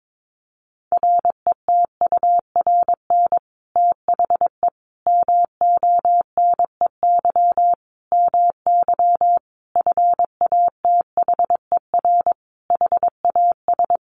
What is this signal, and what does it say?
Morse code, heard as a keyed tone, about 22 wpm, 700 Hz. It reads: RETURN THE MONEY MY FATHER HAS
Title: Sentences from Top 1200 Words 22wpm